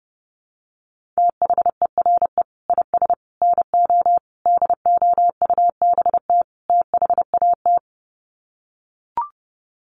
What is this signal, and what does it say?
Morse code, heard as a keyed tone, about 30 wpm, 700 Hz. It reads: THERE IS NO DOUBT THAT
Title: Sets of 5 Words 30wpm